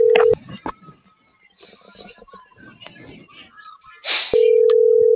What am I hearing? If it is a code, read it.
Ringback tone (North America), 440 + 480 Hz.